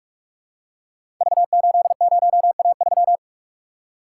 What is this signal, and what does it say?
Morse code, heard as a keyed tone, about 44 wpm, 700 Hz. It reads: V80A3
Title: Code Groups - 5 Letter:Numbers - 45wpm